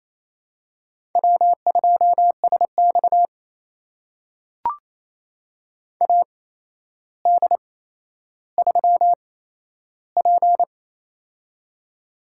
Morse code, heard as a keyed tone, about 28 wpm, spaced as for 12 wpm, 700 Hz. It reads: W2SX AD3P
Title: USA Call Signs - 4 Characters - 8x Character Spacing - 28wpm